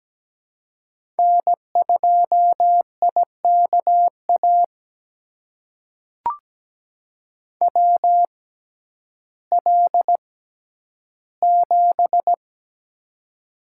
Morse code, heard as a keyed tone, about 17 wpm, spaced as for 10 wpm, 700 Hz. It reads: N2IKA WL7
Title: USA Call Signs - 6x Character Spacing - 17wpm